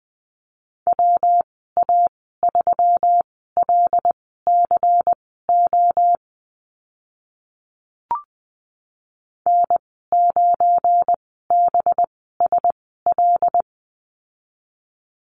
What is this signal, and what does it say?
Morse code, heard as a keyed tone, about 20 wpm, spaced as for 14 wpm, 700 Hz. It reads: WA3LCO N9BSL
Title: 200 Lesson 32g - Letter k - Call Signs 20wpm